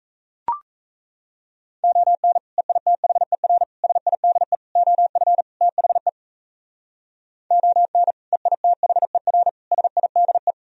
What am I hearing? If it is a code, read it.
Morse code, heard as a keyed tone, about 36 wpm, 700 Hz. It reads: ON EITHER SIDE OF THE ON EITHER SIDE